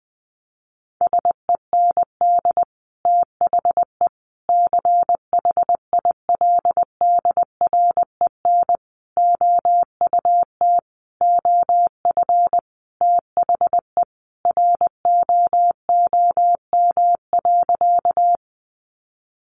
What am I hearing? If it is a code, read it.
Morse code, heard as a keyed tone, about 20 wpm, 700 Hz. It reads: SEND THE CHILDREN OUT OF THE ROOM.
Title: Sentences from Top 1500 Words 20wpm